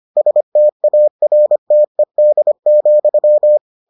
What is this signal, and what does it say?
Morse code, heard as a keyed tone, about 25 wpm, 600 Hz. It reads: STARTED,